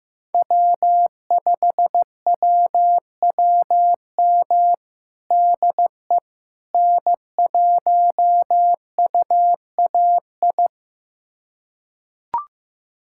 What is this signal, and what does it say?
Morse code, heard as a keyed tone, about 15 wpm, 700 Hz. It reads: W5WWM DE N1UAI
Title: USA Callsign DE Callsign - 5 Characters - 15wpm